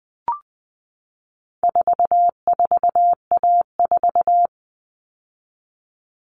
Morse code, heard as a keyed tone, about 20 wpm, 700 Hz. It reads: 44A4
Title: All Cut Numbers - 4-Digit 20wpm